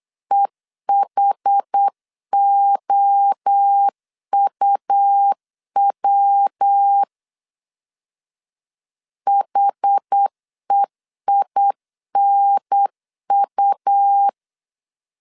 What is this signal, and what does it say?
Morse code, heard as a keyed tone, about 8 wpm, 800 Hz. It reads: EHOUW HEINU
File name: CW / DET / 1 3 1500C S C 8wpm 4214